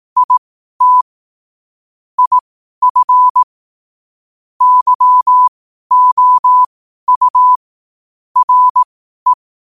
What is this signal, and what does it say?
Morse code, heard as a keyed tone, about 18 wpm, spaced as for 12 wpm, 1000 Hz. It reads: IT IF YOU RE
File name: qotdfiles / 12wpm